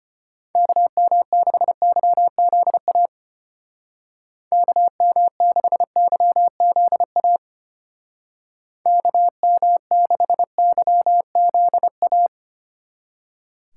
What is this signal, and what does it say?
Morse code, heard as a keyed tone, about 32 wpm, 700 Hz. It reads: KM6YZA KM6YZA KM6YZA